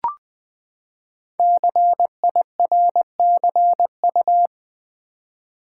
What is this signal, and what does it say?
Morse code, heard as a keyed tone, about 20 wpm, 700 Hz. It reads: CIRCU
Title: Build A Word - 5 Letters At A Time - 12-Letter Words - 20wpm